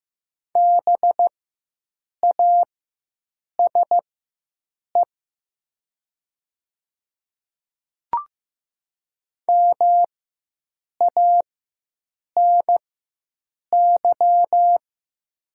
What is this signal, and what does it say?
Morse code, heard as a keyed tone, about 15 wpm, spaced as for 7 wpm, 700 Hz. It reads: BASE MANY
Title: 4x Character Spacing - Top 300 Words - 15wpm